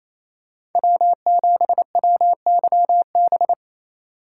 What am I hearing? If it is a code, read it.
Morse code, heard as a keyed tone, about 28 wpm, 700 Hz. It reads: W7WYB
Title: USA Call Signs - 5 Characters - 4x Character Spacing - 28wpm